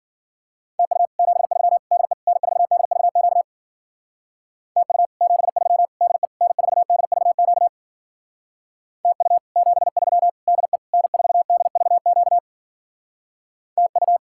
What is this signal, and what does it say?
Morse code, heard as a keyed tone, about 48 wpm, 700 Hz. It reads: TU 73 DE N4DVQ TU 73 DE N4DVQ TU 73 DE N4DVQ TU
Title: Speed-Racing Common QSO Phrases - 40wpm